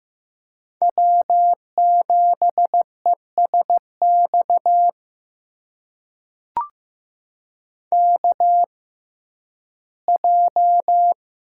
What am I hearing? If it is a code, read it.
Morse code, heard as a keyed tone, about 15 wpm, spaced as for 11 wpm, 700 Hz. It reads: W7ESX KJ
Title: USA Call Signs - 6x Character Spacing - 15wpm